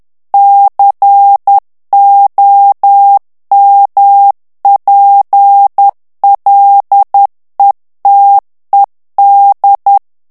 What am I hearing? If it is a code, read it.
Morse code, heard as a keyed tone, about 10 wpm, 800 Hz. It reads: COMPLETED
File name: ANGLAIS texte clair 3 vitesse 10 mots minute